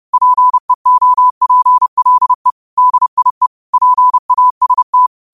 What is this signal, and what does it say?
Morse code, heard as a keyed tone, about 30 wpm, 1000 Hz. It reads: PEOPLE DIE PAST
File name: 30wpm